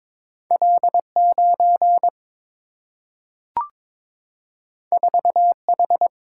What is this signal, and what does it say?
Morse code, heard as a keyed tone, about 23 wpm, 700 Hz. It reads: L9 4H
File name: practice / 22wpm / Two Letter-Number Combinations - 22wpm